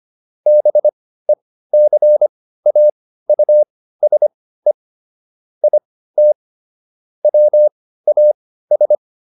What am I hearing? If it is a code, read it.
Morse code, heard as a keyed tone, about 25 wpm, spaced as for 15 wpm, 600 Hz. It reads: BECAUSE IT WAS